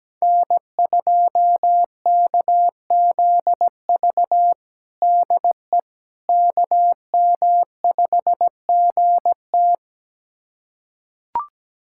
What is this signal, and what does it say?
Morse code, heard as a keyed tone, about 17 wpm, 700 Hz. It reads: N2KZV DE KM5GT
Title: USA Callsign DE Callsign - 5 Characters - 17wpm